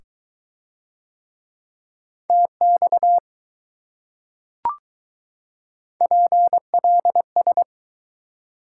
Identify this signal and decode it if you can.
Morse code, heard as a keyed tone, about 21 wpm, 700 Hz. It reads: TX PLS